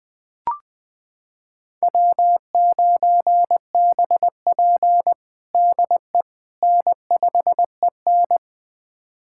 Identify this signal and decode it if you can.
Morse code, heard as a keyed tone, about 21 wpm, 700 Hz. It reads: W9BP DE N5EN
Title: USA Callsign DE Callsign - 4 Characters - 20wpm